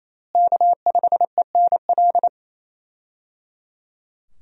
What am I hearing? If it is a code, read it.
Morse code, heard as a keyed tone, about 28 wpm, 700 Hz. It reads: K5ENL
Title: USA Call Signs - 5 Characters - Repeated 3x - 28wpm